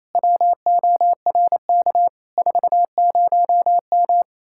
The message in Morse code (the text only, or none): WORK 40M